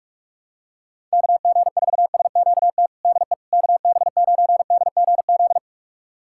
Morse code, heard as a keyed tone, about 46 wpm, 700 Hz. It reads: KG4SQT DE KB9DGZ
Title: USA Callsign DE Callsign - 6 Characters - 45wpm